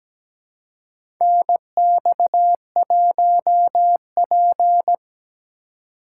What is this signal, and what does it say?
Morse code, heard as a keyed tone, about 17 wpm, 700 Hz. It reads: NX1P
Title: USA Call Signs - 4 Characters - 3x Character Spacing - 17wpm